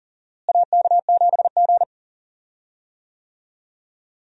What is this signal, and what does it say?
Morse code, heard as a keyed tone, about 40 wpm, 700 Hz. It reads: AK7G
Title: USA Call Signs - 4 Characters - Repeated 2x - 40wpm